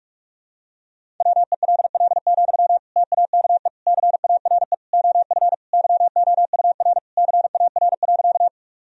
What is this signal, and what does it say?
Morse code, heard as a keyed tone, about 46 wpm, 700 Hz. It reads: WELL, TAKE CARE OF YOUR CAR.
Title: Sentences from Top 1000 Words - Review Part 1 - 45wpm